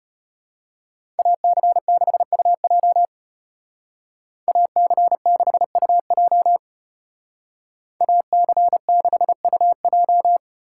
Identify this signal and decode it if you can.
Morse code, heard as a keyed tone, about 30 wpm, 700 Hz. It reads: AC6UJ AC6UJ AC6UJ
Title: Speed Racing - USA 2x2 Call Signs 25wpm